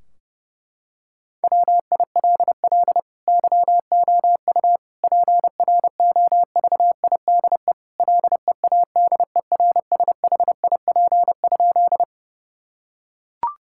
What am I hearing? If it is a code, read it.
Morse code, heard as a keyed tone, about 32 wpm, 700 Hz. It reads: WILL YOU PROVIDE LEADERSHIP?
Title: Sentences from Top 1600 Words 30wpm